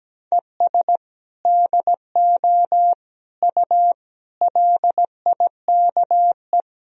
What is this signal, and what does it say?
Morse code, heard as a keyed tone, about 17 wpm, 700 Hz. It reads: ES DO U LIKE
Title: Sentences from Top 100 Words and Abbreviations 17wpm